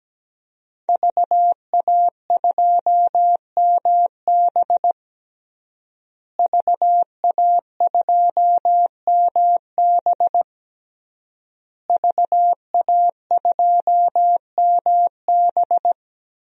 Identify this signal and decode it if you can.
Morse code, heard as a keyed tone, about 17 wpm, 700 Hz. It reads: VA2MB VA2MB VA2MB
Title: DXCC-100 Call Signs - 5 Characters - Repeated 3x - 17wpm